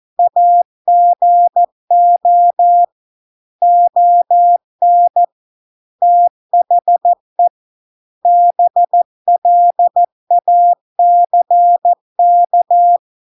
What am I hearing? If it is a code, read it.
Morse code, heard as a keyed tone, about 14 wpm, 700 Hz. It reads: AGO ON THE BLACK